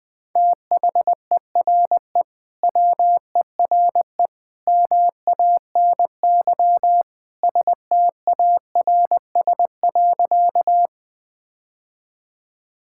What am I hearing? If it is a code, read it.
Morse code, heard as a keyed tone, about 21 wpm, 700 Hz. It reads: THERE WERE MANY STARS.
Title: Sentences with 2-Word Phrases - Spoken Beginning - 20wpm